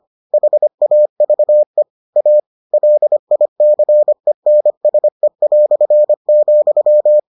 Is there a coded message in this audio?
Yes: Morse code from 0.3 to 7.2 s, heard as a keyed tone, about 25 wpm, 600 Hz. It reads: HAVE A LICENSE",